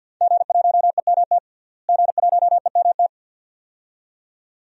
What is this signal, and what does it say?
Morse code, heard as a keyed tone, about 50 wpm, 700 Hz. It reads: G1EMT G1EMT
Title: DXCC-100 Call Signs - 5 Characters - Repeated 2x - 50wpm